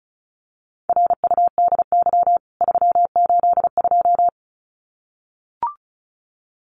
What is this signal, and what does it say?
Morse code, heard as a keyed tone, about 34 wpm, 700 Hz. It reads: RUDY 382
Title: CWT Exchange 35wpm